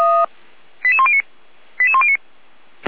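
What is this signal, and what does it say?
Five-tone selective call (ZVEI), heard as quick runs of five tones, tones of about 70 ms. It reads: 88109 88109
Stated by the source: ZVEI five-tone call received on SDR.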